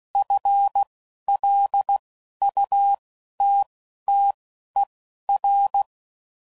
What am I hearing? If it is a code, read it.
Morse code, heard as a keyed tone, about 16 wpm, spaced as for 11 wpm, 800 Hz. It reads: FLUTTER
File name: Prefixes / FLs & THs / FL Runs